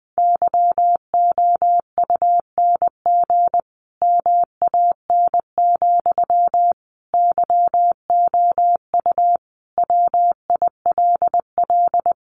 Morse code, heard as a keyed tone, about 20 wpm, 700 Hz. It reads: YOUNG MAN, YOU WILL
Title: Sentences from Top 400 Words 20wpm